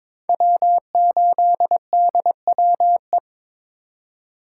Morse code, heard as a keyed tone, about 22 wpm, 700 Hz. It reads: W8DWE
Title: USA Call Signs - 2x Character Spacing - 22wpm